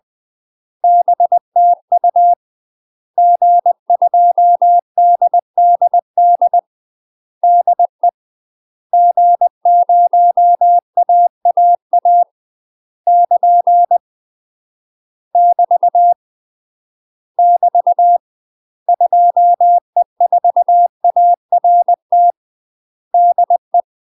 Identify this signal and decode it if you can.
Morse code, heard as a keyed tone, about 20 wpm, 700 Hz. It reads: BTU G2DDD DE G0AAA <KN> <BT> <BT> 2E4ART DE
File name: UK MASTER QSOs TRAINING/20WPM